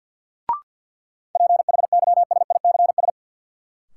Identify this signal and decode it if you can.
Morse code, heard as a keyed tone, about 50 wpm, 700 Hz. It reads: PHYSICS
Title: Sentences from Top 1500 Words - New Word Drill Alt - 50wpm